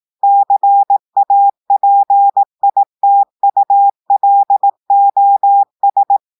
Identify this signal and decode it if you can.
Morse code, heard as a keyed tone, about 18 wpm, 800 Hz. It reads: CAPITULOS
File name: audios / portugais / portugais texte en clair 1 vitesse 18 mots minutes